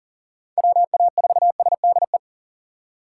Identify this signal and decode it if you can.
Morse code, heard as a keyed tone, about 40 wpm, 700 Hz. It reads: WA4SDE
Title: USA Call Signs - 6 Characters - 3x Character Spacing - 40wpm